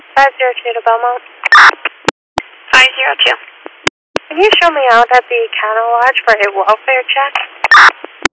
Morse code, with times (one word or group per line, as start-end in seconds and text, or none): none